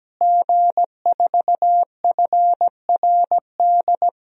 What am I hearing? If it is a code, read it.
Morse code, heard as a keyed tone, about 17 wpm, 700 Hz. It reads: G4FRD
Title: DXCC-100 Call Signs - 5 Characters - Repeated 2x - 17wpm